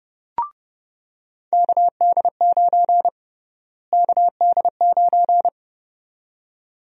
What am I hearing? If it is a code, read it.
Morse code, heard as a keyed tone, about 32 wpm, 700 Hz. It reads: KD9 KD9
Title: USA Call Sign Prefixes - Repeated 2x - 30wpm